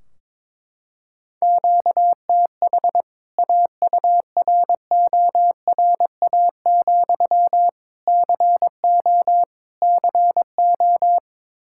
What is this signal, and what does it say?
Morse code, heard as a keyed tone, about 22 wpm, 700 Hz. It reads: QTH AURORA, CO CO
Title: K1USN SST QSO Practice 22wpm